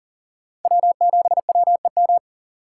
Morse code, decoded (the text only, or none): W7WEM